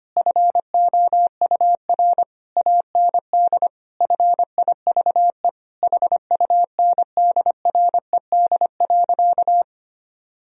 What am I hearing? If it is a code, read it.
Morse code, heard as a keyed tone, about 25 wpm, 700 Hz. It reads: FOUR AND FIVE HUNDRED.
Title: Sentences from Top 1000 Words - Review Part 2 - 25wpm